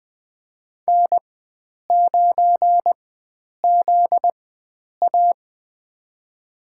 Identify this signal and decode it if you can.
Morse code, heard as a keyed tone, about 21 wpm, spaced as for 9 wpm, 700 Hz. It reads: N9ZA
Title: USA Call Signs - 4 Characters - 4x Character Spacing - 20wpm